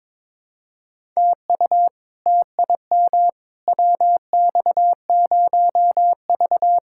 Morse code, heard as a keyed tone, about 22 wpm, 700 Hz. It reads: TU TIM WX0V
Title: K1USN SST QSO Practice - 22wpm